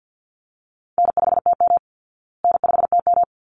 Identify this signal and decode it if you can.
Morse code, heard as a keyed tone, about 48 wpm, 700 Hz. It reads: N5TM N5TM
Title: Call Signs Repeated 3x - 50wpm